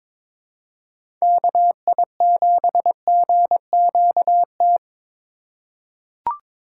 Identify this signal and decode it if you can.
Morse code, heard as a keyed tone, about 22 wpm, 700 Hz. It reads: KI7GQT
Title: USA Call Signs - 8x Character Spacing - 22wpm